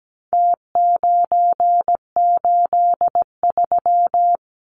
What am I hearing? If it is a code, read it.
Morse code, heard as a keyed tone, about 17 wpm, 700 Hz. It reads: T983
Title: Numbers - Contest Style - 4-Digit 17wpm